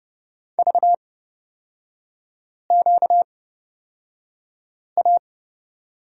Morse code, heard as a keyed tone, about 28 wpm, spaced as for 5 wpm, 700 Hz. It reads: VQA
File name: Lesson 38e - Letter q - Three Characters 30wpm